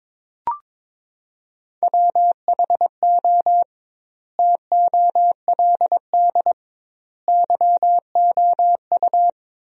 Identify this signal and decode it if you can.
Morse code, heard as a keyed tone, about 22 wpm, 700 Hz. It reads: WHO TOLD YOU
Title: 2x Word-Spacing - Sentences from Top 400 Words - 22wpm